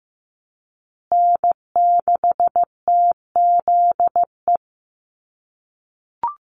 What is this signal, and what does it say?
Morse code, heard as a keyed tone, about 15 wpm, 700 Hz. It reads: N6TZE
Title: USA Call Signs Repeated 2x 15wpm